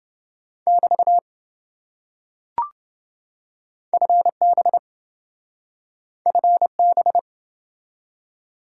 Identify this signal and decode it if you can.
Morse code, heard as a keyed tone, about 26 wpm, 700 Hz. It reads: <BT> FB FB